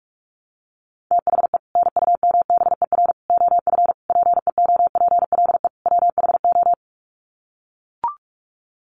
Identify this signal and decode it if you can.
Morse code, heard as a keyed tone, about 44 wpm, 700 Hz. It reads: THE NUMBER OF PEOPLE WHO
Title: Sets of 5 Words Encore 45wpm